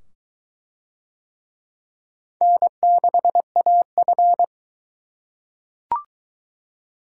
Morse code, heard as a keyed tone, about 22 wpm, 700 Hz. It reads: N6AF